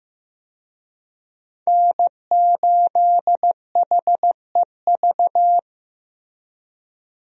Morse code, heard as a keyed tone, about 16 wpm, 700 Hz. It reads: N8HEV
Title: USA Call Signs - 5 Characters - Repeated 3x - 15wpm